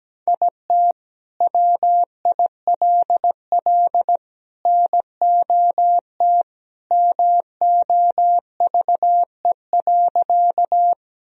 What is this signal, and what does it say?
Morse code, heard as a keyed tone, about 17 wpm, 700 Hz. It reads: IT WILL NOT MOVE.
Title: Sentences from Top 1000 Words - Review Part 1 - 17wpm